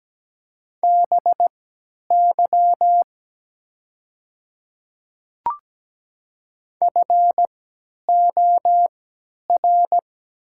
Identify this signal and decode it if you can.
Morse code, heard as a keyed tone, about 17 wpm, spaced as for 10 wpm, 700 Hz. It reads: BY FOR
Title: 3x Character Spacing - Top 100 Words - 17wpm